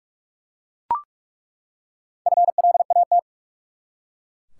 Morse code, heard as a keyed tone, about 42 wpm, 700 Hz. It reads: FLAT
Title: Four Letter Words - 45wpm